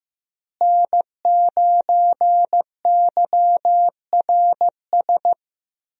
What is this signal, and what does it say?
Morse code, heard as a keyed tone, about 15 wpm, 700 Hz. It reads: N9YRS
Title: USA Call Signs - 5 Characters - Repeated 3x - 15wpm